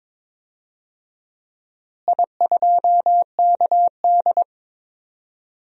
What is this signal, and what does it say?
Morse code, heard as a keyed tone, about 21 wpm, 700 Hz. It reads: I2KD
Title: DXCC-100 Call Signs - 4 Characters - Repeated 2x - 22wpm